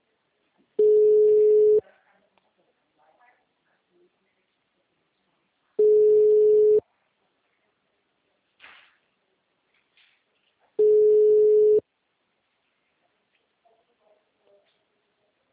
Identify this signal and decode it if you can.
Ringback tone (Europe, 425 Hz).
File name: out- -2015- - - .520